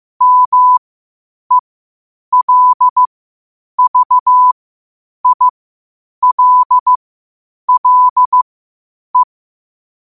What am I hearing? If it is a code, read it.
Morse code, heard as a keyed tone, about 15 wpm, spaced as for 8 wpm, 1000 Hz. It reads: MELVILLE